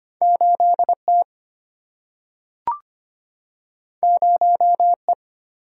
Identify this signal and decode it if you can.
Morse code, heard as a keyed tone, about 25 wpm, 700 Hz. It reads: 8T 0E
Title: Single-Letter Single-Number 25wpm